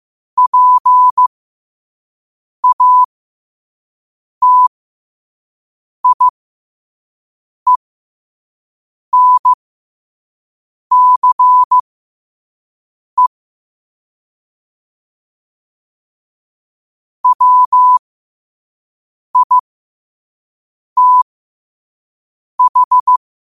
Morse code, heard as a keyed tone, about 15 wpm, spaced as for 5 wpm, 1000 Hz. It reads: PATIENCE WITH
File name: qotdfiles / 5wpm